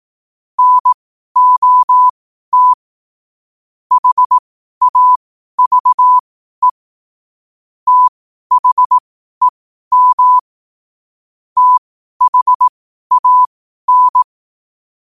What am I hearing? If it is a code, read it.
Morse code, heard as a keyed tone, about 18 wpm, spaced as for 12 wpm, 1000 Hz. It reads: NOT HAVE THEM THAN
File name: qotdfiles / 12wpm